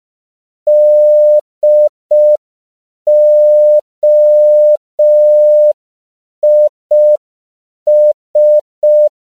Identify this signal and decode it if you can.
Morse code, heard as a keyed tone, about 5 wpm, 600 Hz. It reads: DOIS